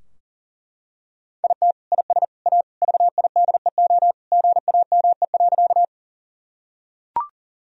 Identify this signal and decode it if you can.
Morse code, heard as a keyed tone, about 37 wpm, 690 Hz. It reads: IT IS A VIDEO GAME.
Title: Sentences from Top 1000 Words - Review Part 4 - 40wpm